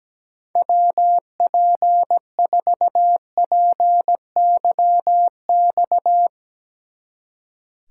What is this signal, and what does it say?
Morse code, heard as a keyed tone, about 18 wpm, 700 Hz. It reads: WP4PYX
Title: Speed Racing - USA 2x3 Call Signs 17wpm